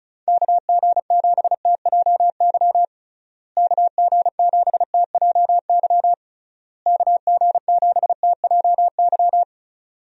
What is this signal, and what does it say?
Morse code, heard as a keyed tone, about 34 wpm, 700 Hz. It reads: KG7TJY KG7TJY KG7TJY
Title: USA Call Signs - 6 Characters - Repeated 3x - 35wpm